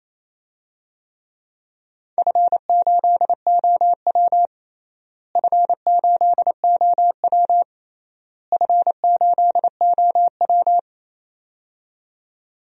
Morse code, heard as a keyed tone, about 28 wpm, 700 Hz. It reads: F8OW F8OW F8OW
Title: DXCC-100 Call Signs - 4 Characters - Repeated 3x - 28wpm